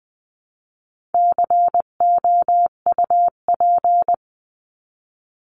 Morse code, heard as a keyed tone, about 20 wpm, 700 Hz. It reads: COUP
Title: 259 Lesson 42d - Words 18-20wpm